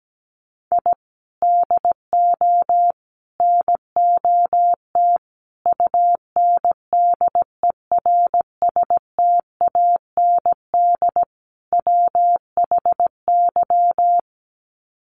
Morse code, heard as a keyed tone, about 17 wpm, 700 Hz. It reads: I DO NOT UNDERSTAND WHY
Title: Sets of 5 Words Encore 17wpm